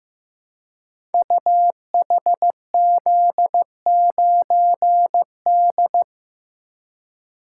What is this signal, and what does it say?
Morse code, heard as a keyed tone, about 15 wpm, 700 Hz. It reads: UHZ9D
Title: Code Groups - 5 Letter:Numbers - 15wpm